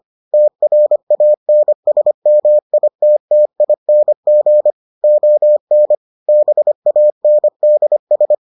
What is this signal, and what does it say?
Morse code, heard as a keyed tone, about 25 wpm, 600 Hz. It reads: TRANSMITTING ON BANDS